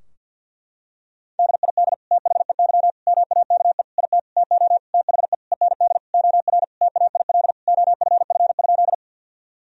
Morse code, heard as a keyed tone, about 52 wpm, 700 Hz. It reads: DID THEY MAKE IT TO THE END OR TAIL OFF?
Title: Sentences from Top 1000 Words - Review Part 3 - 50wpm